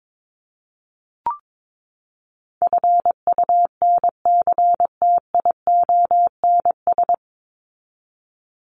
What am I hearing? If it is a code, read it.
Morse code, heard as a keyed tone, about 22 wpm, 700 Hz. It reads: FUNCTIONS
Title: ES:S Suffix - Top 100 Words with Base Words 22wpm